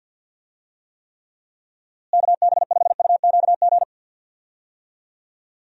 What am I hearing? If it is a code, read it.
Morse code, heard as a keyed tone, about 46 wpm, 700 Hz. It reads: KB5UQG
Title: USA Call Signs - 6 Characters - Repeated 3x - 50wpm